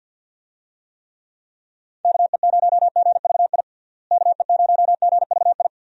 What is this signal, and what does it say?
Morse code, heard as a keyed tone, about 50 wpm, 700 Hz. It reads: KE0GVI KE0GVI
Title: USA Call Signs - 6 Characters - Repeated 2x - 50wpm